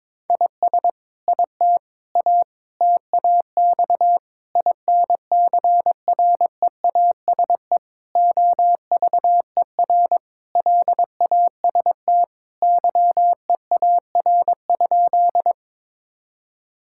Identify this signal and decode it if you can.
Morse code, heard as a keyed tone, about 21 wpm, 700 Hz. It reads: IS IT A TAX INCREASE OVER LAST YEAR?
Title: Sentences from Top 1000 Words - Review Part 5 - 22wpm